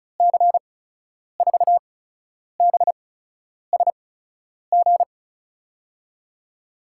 Morse code, heard as a keyed tone, about 36 wpm, spaced as for 10 wpm, 700 Hz. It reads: C4BSG
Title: USA Call Signs - 6 Characters - 8x Character Spacing - 35wpm